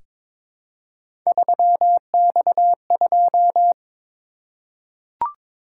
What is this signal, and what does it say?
Morse code, heard as a keyed tone, about 23 wpm, 700 Hz. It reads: 3X2
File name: practice / 22wpm / Three Letter-Number Combinations - 22wpm